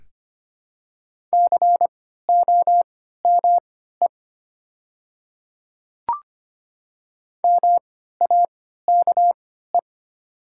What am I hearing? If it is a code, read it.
Morse code, heard as a keyed tone, about 25 wpm, spaced as for 14 wpm, 700 Hz. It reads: COME MAKE
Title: 3x Character Spacing - Top 100 Words - 25wpm